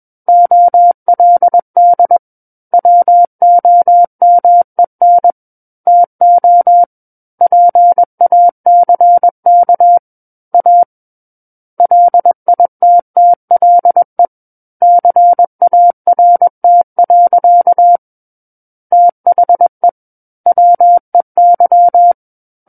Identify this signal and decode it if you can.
Morse code, heard as a keyed tone, about 21 wpm, 700 Hz. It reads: OLD WOMEN TO PACK A LITTLE CART. THE WEY
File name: Book One / 12-Chapter 12 - 21 WPM